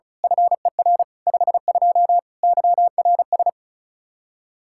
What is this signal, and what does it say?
Morse code, heard as a keyed tone, about 34 wpm, 700 Hz. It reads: FER 52 YRS
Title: Practice QSO - Exchange 3 - More Info - 35wpm